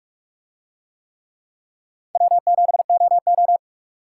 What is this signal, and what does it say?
Morse code, heard as a keyed tone, about 44 wpm, 700 Hz. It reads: W7OO